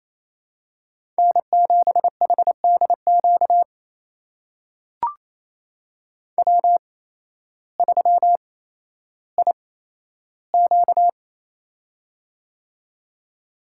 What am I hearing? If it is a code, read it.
Morse code, heard as a keyed tone, about 28 wpm, spaced as for 13 wpm, 700 Hz. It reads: N7HDQ W3IQ
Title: USA Call Signs - 8x Character Spacing - 28wpm